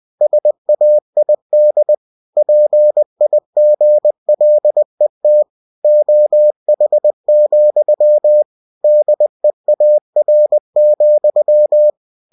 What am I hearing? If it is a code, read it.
Morse code, heard as a keyed tone, about 20 wpm, 600 Hz. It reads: SAID PIGLET OH, DEAR,
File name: cw-books / winnie-the-pooh / winnie-20wpm-page-0053